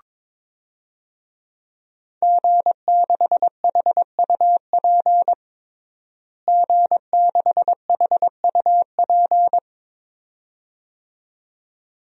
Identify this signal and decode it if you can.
Morse code, heard as a keyed tone, about 22 wpm, 700 Hz. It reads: G6HUP G6HUP
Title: DXCC-100 Call Signs - 5 Characters - Repeated 2x - 22wpm